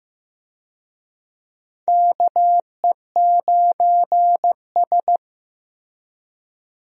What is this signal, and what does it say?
Morse code, heard as a keyed tone, about 15 wpm, 700 Hz. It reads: KE9S
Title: USA Call Signs - 4 Characters - Repeated 2x - 15wpm